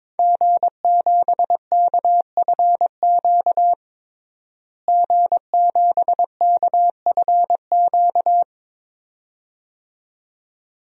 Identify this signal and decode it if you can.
Morse code, heard as a keyed tone, about 22 wpm, 700 Hz. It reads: G7KFQ G7KFQ
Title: DXCC-100 Call Signs - 5 Characters - Repeated 2x - 22wpm